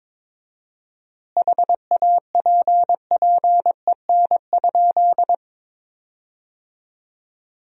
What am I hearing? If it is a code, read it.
Morse code, heard as a keyed tone, about 22 wpm, 690 Hz. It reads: HAPPEN?
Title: 4x Word-Spacing - Sentences from Top 400 Words - 22wpm